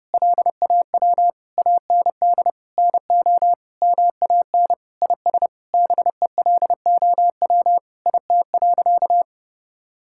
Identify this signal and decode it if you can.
Morse code, heard as a keyed tone, about 30 wpm, 700 Hz. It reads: LAW AND NO MAN IS BELOW IT.
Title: Sentences from Top 1000 Words - Review Part 5 - 30wpm